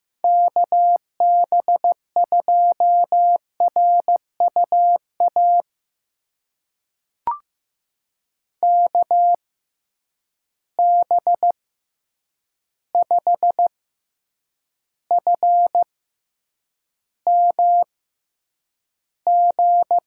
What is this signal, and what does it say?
Morse code, heard as a keyed tone, about 15 wpm, spaced as for 8 wpm, 700 Hz. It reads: KB2RUA KB5FMG
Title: USA Call Signs - 6 Characters - 6x Character Spacing - 15wpm